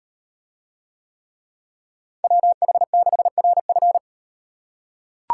Morse code, heard as a keyed tone, about 38 wpm, 700 Hz. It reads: WH6RF